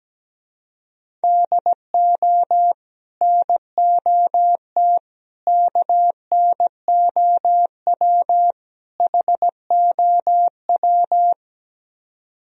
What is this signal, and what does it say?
Morse code, heard as a keyed tone, about 17 wpm, 700 Hz. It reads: DO NOT KNOW HOW
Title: Speed-Racing - Sets of 4 Words 17wpm